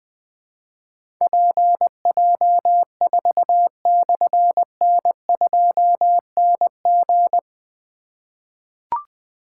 Morse code, heard as a keyed tone, about 20 wpm, 700 Hz. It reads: PJ4/N2NG
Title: Difficult Call Signs 20wpm